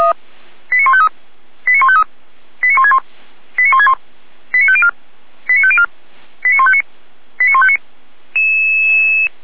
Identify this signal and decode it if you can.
Five-tone selective call (ZVEI), heard as quick runs of five tones, tones of about 70 ms. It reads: 89153 89153 89161 89161 89504 89504 89159 89159
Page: Server at audio.f85360.feuerwehr-orlamuende.org